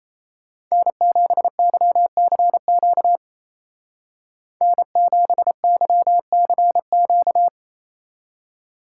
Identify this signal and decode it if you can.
Morse code, heard as a keyed tone, about 34 wpm, 700 Hz. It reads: N7YCQ N7YCQ
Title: Speed Racing - USA 1x3 Call Signs 28wpm